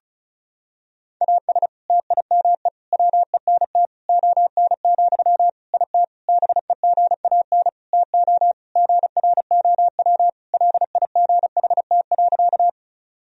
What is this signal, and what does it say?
Morse code, heard as a keyed tone, about 36 wpm, 700 Hz. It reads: AS TIME WENT ON, IT BEGAN TO GROW LIGHT.
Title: Sentences from Top 1000 Words - Review Part 1 - 35wpm